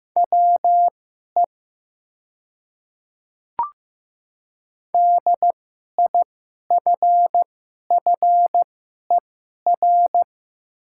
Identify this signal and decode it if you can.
Morse code, heard as a keyed tone, about 15 wpm, spaced as for 11 wpm, 700 Hz. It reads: WE DIFFER
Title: 2x Character Spacing - Top 300 Words - 15wpm